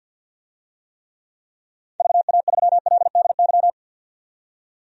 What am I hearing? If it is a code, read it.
Morse code, heard as a keyed tone, about 50 wpm, 700 Hz. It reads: VA3LDY
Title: DXCC-100 Call Signs - 6 Characters - 50wpm